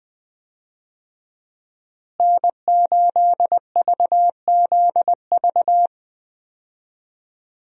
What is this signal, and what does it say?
Morse code, heard as a keyed tone, about 20 wpm, 700 Hz. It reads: N8VZV
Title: USA Call Signs - 5 Characters - Repeated 3x - 20wpm